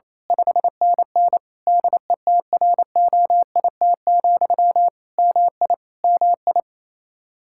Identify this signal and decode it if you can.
Morse code, heard as a keyed tone, about 28 wpm, 700 Hz. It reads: 5NN DETROIT, MI MI
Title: Practice QSO - Exchange 1 - Beginning - 28wpm